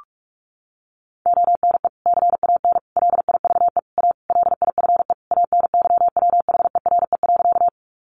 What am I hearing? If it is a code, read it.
Morse code, heard as a keyed tone, about 44 wpm, 700 Hz. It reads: ONE CAN LIVE A LIFE ANYWHERE.
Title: Sentences from Top 1400 Words 45wpm